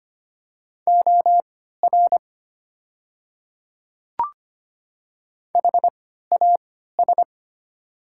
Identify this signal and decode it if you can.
Morse code, heard as a keyed tone, about 25 wpm, spaced as for 14 wpm, 700 Hz. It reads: OR HAS
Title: 3x Character Spacing - Top 200 Words - 25wpm